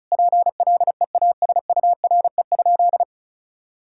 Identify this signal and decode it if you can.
Morse code, heard as a keyed tone, about 34 wpm, 700 Hz. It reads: PLEASURE?
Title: Sentences from Top 1600 Words 35wpm